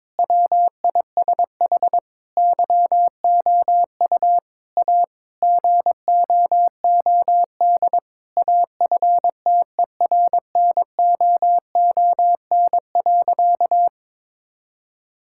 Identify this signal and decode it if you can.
Morse code, heard as a keyed tone, about 22 wpm, 700 Hz. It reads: WISH YOU A GOOD AFTERNOON.
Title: Sentences from Top 1600 Words 22wpm